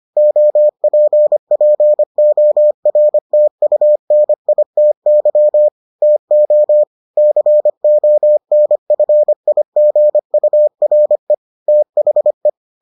Morse code, heard as a keyed tone, about 25 wpm, 600 Hz. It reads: OPPORTUNITY TO CONFIGURE THE